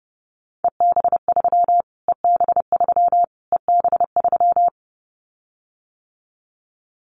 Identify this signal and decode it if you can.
Morse code, heard as a keyed tone, about 30 wpm, 700 Hz. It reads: EB3 EB3 EB3
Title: Callsign Prefixes for Top 25 DXCC - Repeated 3x - 30wpm